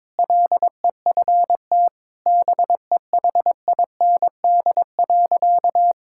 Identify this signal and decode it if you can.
Morse code, heard as a keyed tone, about 21 wpm, 700 Hz. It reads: LEFT BEHIND.
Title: Sentences from Top 1000 Words - Review Part 5 - 22wpm